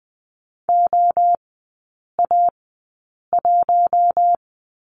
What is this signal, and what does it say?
Morse code, heard as a keyed tone, about 20 wpm, spaced as for 8 wpm, 700 Hz. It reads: OA1
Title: 041 Lesson 8e - Number 1 - Three Characters 20wpm